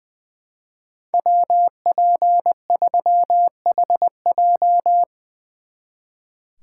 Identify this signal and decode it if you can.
Morse code, heard as a keyed tone, about 23 wpm, 700 Hz. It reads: WP3HJ
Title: Speed Racing - USA 2x2 Call Signs 20wpm